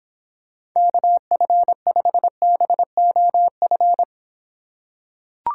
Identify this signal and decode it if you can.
Morse code, heard as a keyed tone, about 25 wpm, 700 Hz. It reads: KF5BOF